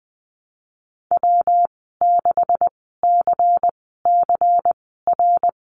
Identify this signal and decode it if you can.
Morse code, heard as a keyed tone, about 20 wpm, spaced as for 15 wpm, 700 Hz. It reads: W6CCR
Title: 147 Lesson 24g - Letter y - Call Signs 20wpm